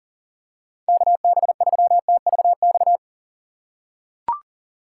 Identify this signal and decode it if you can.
Morse code, heard as a keyed tone, about 37 wpm, 700 Hz. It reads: KB3TVX
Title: USA Call Signs - 6 Characters - 4x Character Spacing - 40wpm